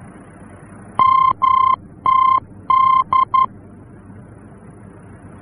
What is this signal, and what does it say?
Morse code, heard as a keyed tone, about 11 wpm, 1050 Hz. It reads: MTD
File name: france / mx / MTD-F-VOR